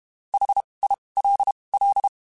Morse code, heard as a keyed tone, about 32 wpm, spaced as for 23 wpm, 800 Hz. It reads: HILL
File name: (spaced) / QSOs Spaced-mp3s / WA7QOX 2 @30 sp